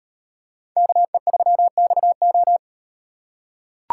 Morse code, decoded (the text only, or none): KE3XO